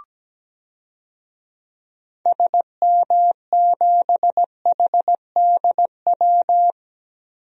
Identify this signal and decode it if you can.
Morse code, heard as a keyed tone, about 17 wpm, 700 Hz. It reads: SM7HDW
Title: DXCC-100 Call Signs - 6 Characters - Repeated 2x - 17wpm